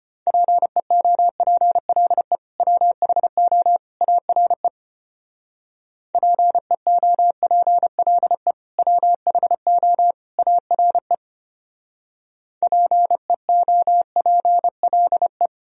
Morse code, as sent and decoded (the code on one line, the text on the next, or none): .--. --- .--. .-.. / .-- .... --- / .- .-. / .--. --- .--. .-.. / .-- .... --- / .- .-. / .--. --- .--. .-.. .
PEOPLE WHO ARE PEOPLE WHO ARE PEOPLE